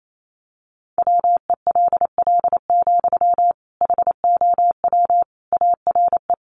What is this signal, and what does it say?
Morse code, heard as a keyed tone, about 28 wpm, 700 Hz. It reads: WELL, HOW ARE
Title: Sentences from Top 200 Words 28wpm